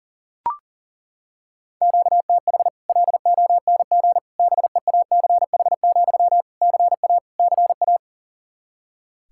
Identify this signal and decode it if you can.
Morse code, heard as a keyed tone, about 38 wpm, 700 Hz. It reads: QTH LONG BEACH, CA CA
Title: Practice QSO - Exchange 1 - Beginning - 40wpm